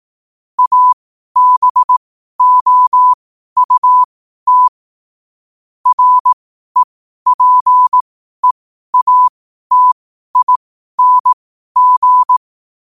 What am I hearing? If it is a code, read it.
Morse code, heard as a keyed tone, about 18 wpm, spaced as for 12 wpm, 1000 Hz. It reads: ABOUT REPEATING
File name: qotdfiles / 12wpm